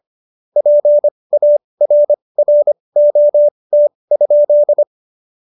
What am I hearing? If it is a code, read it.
Morse code, heard as a keyed tone, about 25 wpm, spaced as for 20 wpm, 590 Hz. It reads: PARROT?
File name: Jokes-25-char-20-eff